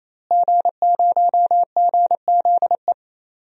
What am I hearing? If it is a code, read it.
Morse code, heard as a keyed tone, about 28 wpm, 700 Hz. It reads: G0GZE
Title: DXCC-100 Call Signs - 5 Characters - Repeated 3x - 28wpm